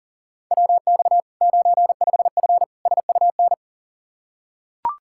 Morse code, heard as a keyed tone, about 40 wpm, 700 Hz. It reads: WX 95F SUN
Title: Practice QSO - Exchange 2 - Info - 40wpm